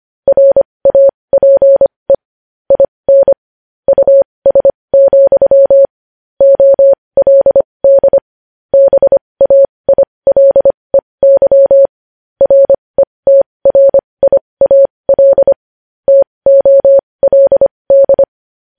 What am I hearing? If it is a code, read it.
Morse code, heard as a keyed tone, about 25 wpm, spaced as for 20 wpm, 550 Hz. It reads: RAPE IN US, OLD BAILEY RETRIAL TOLD
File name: podcast / News headlines-1x-25x20